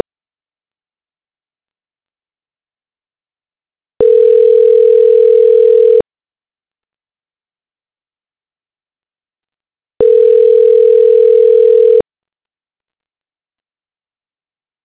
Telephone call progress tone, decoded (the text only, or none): ringback tone